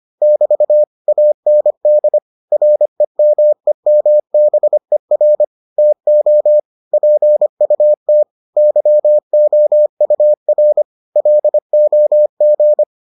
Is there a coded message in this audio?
Yes: Morse code from 0.2 to 12.8 s, heard as a keyed tone, about 25 wpm, 600 Hz. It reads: <BT> AND REMEMBER TO PUT YOUR LOG